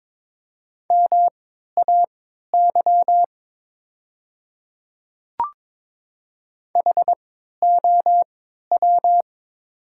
Morse code, heard as a keyed tone, about 22 wpm, spaced as for 12 wpm, 700 Hz. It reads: MAY HOW
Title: 3x Character Spacing - Top 200 Words - 22wpm